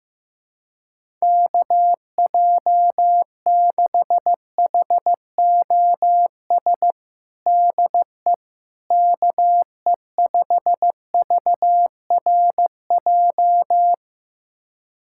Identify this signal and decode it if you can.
Morse code, heard as a keyed tone, about 15 wpm, 700 Hz. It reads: KJ6HOS DE KE5VRJ
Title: USA Callsign DE Callsign - 6 Characters - 15wpm